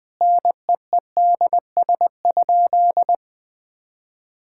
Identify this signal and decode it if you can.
Morse code, heard as a keyed tone, about 20 wpm, 700 Hz. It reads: NEEDS?
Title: Sentences from Top 1000 Words - Review Part 5 - 20wpm